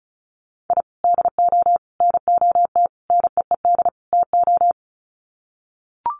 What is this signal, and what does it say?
Morse code, heard as a keyed tone, about 34 wpm, 700 Hz. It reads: I DO NOT NEED TO
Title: Sets of 5 Words 35wpm